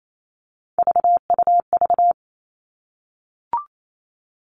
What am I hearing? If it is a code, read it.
Morse code, heard as a keyed tone, about 28 wpm, 700 Hz. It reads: VUV
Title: Three Challenging Letter-Number Combinations 28wpm